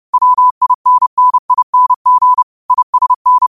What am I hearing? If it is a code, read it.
Morse code, heard as a keyed tone, about 30 wpm, 1000 Hz. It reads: WINNING ISN